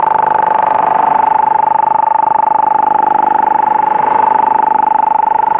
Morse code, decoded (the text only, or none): none